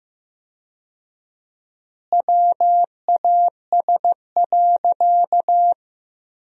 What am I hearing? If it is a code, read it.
Morse code, heard as a keyed tone, about 15 wpm, 700 Hz. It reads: WAS.
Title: 4x Word-Spacing - Sentences from Top 300 Words - 15wpm